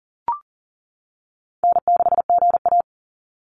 Morse code, heard as a keyed tone, about 40 wpm, 700 Hz. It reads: N6GA
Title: Easy Call Signs - 40wpm